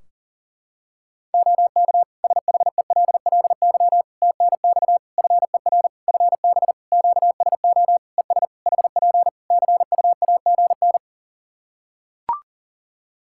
Morse code, heard as a keyed tone, about 40 wpm, 700 Hz. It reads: OK SHELLY TNX FER FB QSO ES HP CUAGN
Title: Practice QSO - Exchange 4 - Ending - 40wpm